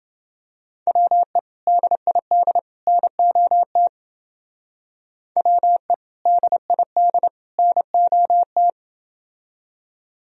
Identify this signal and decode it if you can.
Morse code, heard as a keyed tone, about 30 wpm, 700 Hz. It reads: WE DID NOT WE DID NOT